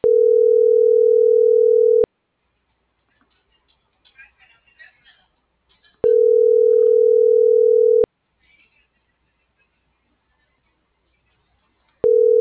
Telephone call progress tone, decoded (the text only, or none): ringback tone